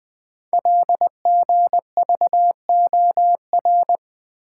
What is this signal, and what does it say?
Morse code, heard as a keyed tone, about 20 wpm, 700 Hz. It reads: LGVOR
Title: Code Groups - 5 Letters - 20wpm